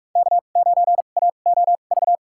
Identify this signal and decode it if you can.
Morse code, heard as a keyed tone, about 46 wpm, spaced as for 33 wpm, 700 Hz. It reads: K9AOV
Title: USA Call Signs - 5 Characters - 2x Character Spacing - 45wpm